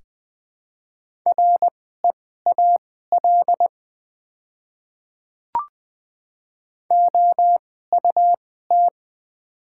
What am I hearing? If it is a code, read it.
Morse code, heard as a keyed tone, about 20 wpm, spaced as for 15 wpm, 700 Hz. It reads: REAL OUT
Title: 2x Character Spacing - Top 300 Words - 20wpm